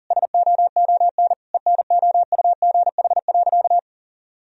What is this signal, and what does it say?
Morse code, heard as a keyed tone, about 40 wpm, 700 Hz. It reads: SOON ENOUGH.
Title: Sentences from Top 1000 Words - Review Part 3 - 40wpm